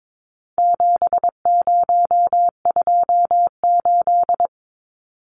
Morse code, heard as a keyed tone, about 22 wpm, 700 Hz. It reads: 7028
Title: Numbers - 4-Digit 22wpm